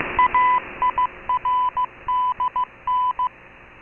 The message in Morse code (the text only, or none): AIRDN